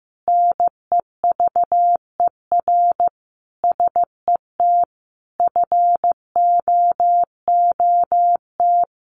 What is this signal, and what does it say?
Morse code, heard as a keyed tone, about 15 wpm, 700 Hz. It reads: NEVER SET FOOT
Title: Sentences from Top 1500 Words 15wpm